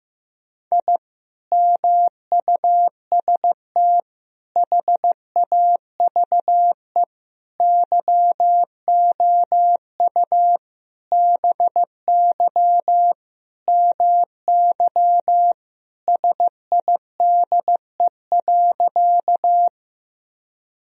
Morse code, heard as a keyed tone, about 15 wpm, 700 Hz. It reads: I MUST HAVE YOU BY MY SIDE.
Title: Sentences from Top 1000 Words - Review Part 1 - 15wpm